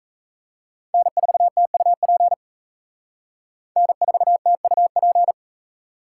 Morse code, heard as a keyed tone, about 35 wpm, 700 Hz. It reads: N4TUP N4TUP